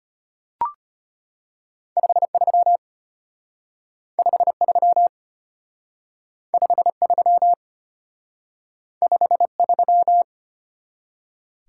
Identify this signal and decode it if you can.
Morse code, heard as a keyed tone, about 33 wpm, 700 Hz. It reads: 53 53 53 53